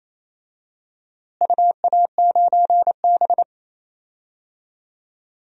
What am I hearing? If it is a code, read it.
Morse code, heard as a keyed tone, about 27 wpm, 700 Hz. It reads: UA9B
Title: DXCC-100 Call Signs - 4 Characters - Repeated 2x - 28wpm